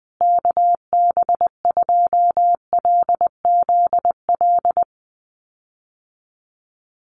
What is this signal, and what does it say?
Morse code, heard as a keyed tone, about 20 wpm, 700 Hz. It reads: KB2LZL
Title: USA Call Signs Repeated 3x 20wpm